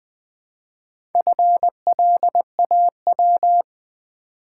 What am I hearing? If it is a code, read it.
Morse code, heard as a keyed tone, about 20 wpm, 700 Hz. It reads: FLAW
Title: Build A Word - 2 Letters At A Time - 4-Letter Words - 20wpm